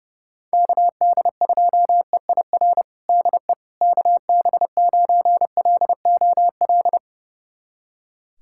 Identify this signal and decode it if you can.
Morse code, heard as a keyed tone, about 32 wpm, 700 Hz. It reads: KD2EIR DE KB9LOL
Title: USA Callsign DE Callsign - 6 Characters - 30wpm